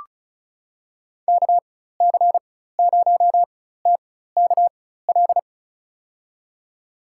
Morse code, heard as a keyed tone, about 34 wpm, spaced as for 16 wpm, 700 Hz. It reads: KC0TKL
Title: USA Call Signs - 6 Characters - 4x Character Spacing - 35wpm